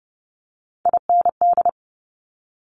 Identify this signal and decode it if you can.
Morse code, heard as a keyed tone, about 30 wpm, 700 Hz. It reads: IND
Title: Multi-letter Phonemes Quick Fire 30wpm